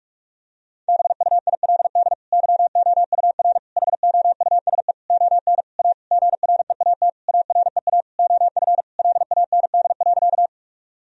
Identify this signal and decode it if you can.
Morse code, heard as a keyed tone, about 46 wpm, 700 Hz. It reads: BUILD YOUR HOUSE ON A GREAT AREA OF LAND.
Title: Sentences from Top 1000 Words - Review Part 2 - 45wpm